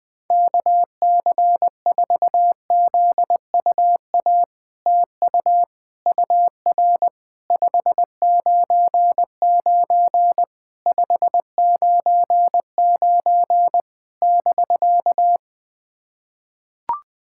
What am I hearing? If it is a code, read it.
Morse code, heard as a keyed tone, about 21 wpm, 700 Hz. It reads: KC4ZUA TU UR 599 599 <BK>
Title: POTA QSO Practice - 20wpm